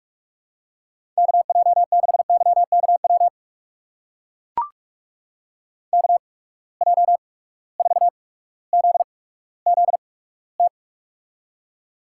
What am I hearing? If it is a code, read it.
Morse code, heard as a keyed tone, about 44 wpm, spaced as for 19 wpm, 700 Hz. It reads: KJ6YKW KJ4ZZT
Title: USA Call Signs - 6 Characters - 8x Character Spacing - 45wpm